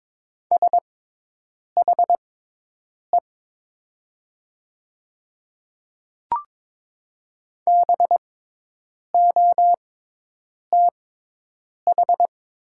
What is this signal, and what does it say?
Morse code, heard as a keyed tone, about 22 wpm, spaced as for 8 wpm, 700 Hz. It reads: SHE BOTH
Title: 6x Character Spacing - Top 300 Words - 22wpm